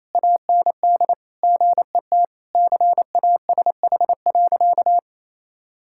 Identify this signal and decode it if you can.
Morse code, heard as a keyed tone, about 28 wpm, 700 Hz. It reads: AND GET CASH.
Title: Sentences from Top 1000 Words - Review Part 5 - 28wpm